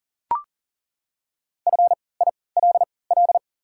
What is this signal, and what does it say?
Morse code, heard as a keyed tone, about 40 wpm, spaced as for 23 wpm, 700 Hz. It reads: FILL
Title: 3x Character Spacing - Top 500 Words - 40wpm